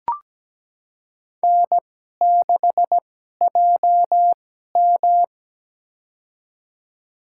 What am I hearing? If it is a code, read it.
Morse code, heard as a keyed tone, about 17 wpm, spaced as for 12 wpm, 700 Hz. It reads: N6JM
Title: USA Call Signs - 4 Characters - 2x Character Spacing - 17wpm